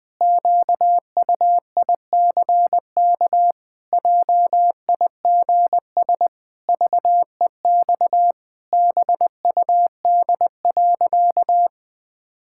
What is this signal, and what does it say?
Morse code, heard as a keyed tone, about 20 wpm, 700 Hz. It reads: QUICK JIGS VEX BUD.
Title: Pangram Challenge v2 - 20wpm